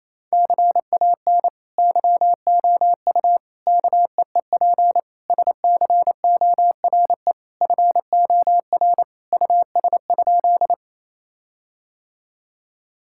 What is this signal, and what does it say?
Morse code, heard as a keyed tone, about 28 wpm, 700 Hz. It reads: CAN YOU KEEP SCORE FOR US?
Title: Sentences from Top 1600 Words 28wpm